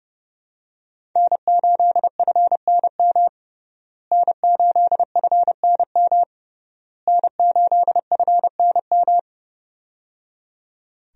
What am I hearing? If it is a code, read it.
Morse code, heard as a keyed tone, about 30 wpm, 700 Hz. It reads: N8FNM N8FNM N8FNM
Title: USA Call Signs - 5 Characters - Repeated 3x - 30wpm